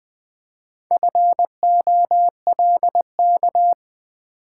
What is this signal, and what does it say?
Morse code, heard as a keyed tone, about 20 wpm, 700 Hz. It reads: FOLK
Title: Build A Word - 1 Letter At A Time - 4-Letter Words - 20wpm